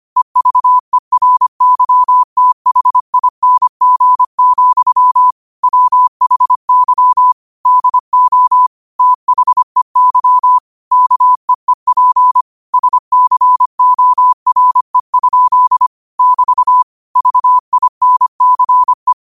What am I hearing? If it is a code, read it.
Morse code, heard as a keyed tone, about 25 wpm, 1000 Hz. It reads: EVERYTHING, WHY DO THEY KEEP SCORE? <BT> VINCE
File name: qotdfiles / 25wpm